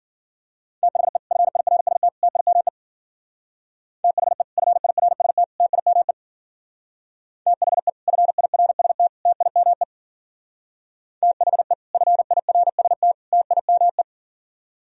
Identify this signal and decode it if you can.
Morse code, heard as a keyed tone, about 50 wpm, 700 Hz. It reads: THE FIRST TIME THE FIRST TIME THE FIRST TIME THE FIRST TIME
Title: Speed-Racing - Sets of 3 Words 40wpm